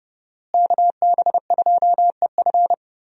Morse code, heard as a keyed tone, about 30 wpm, 700 Hz. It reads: KB2EF
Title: USA Call Signs - 5 Characters - Repeated 2x - 30wpm